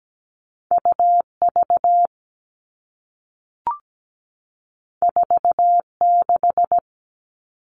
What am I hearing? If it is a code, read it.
Morse code, heard as a keyed tone, about 17 wpm, 700 Hz. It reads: UV 46
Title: All Cut Numbers - 2-Digit 17wpm